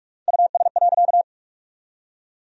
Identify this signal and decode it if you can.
Morse code, heard as a keyed tone, about 47 wpm, 700 Hz. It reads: US.
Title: 3x Word-Spacing - Sentences from Top 200 Words - 45wpm